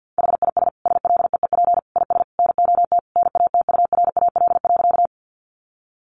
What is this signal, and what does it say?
Morse code, heard as a keyed tone, about 50 wpm, 700 Hz. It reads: HIS SLEEP IS NOT NATURAL.
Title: Sentences from Top 700 Words 50wpm